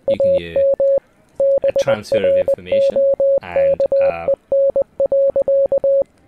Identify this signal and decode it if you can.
Morse code, heard as a keyed tone, about 20 wpm, 560 Hz, with speech. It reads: AM BROKEN.